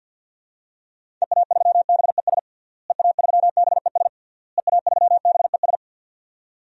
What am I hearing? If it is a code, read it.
Morse code, heard as a keyed tone, about 50 wpm, 700 Hz. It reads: EA3BES EA3BES EA3BES
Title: DXCC-100 Call Signs - 6 Characters - Repeated 3x - 50wpm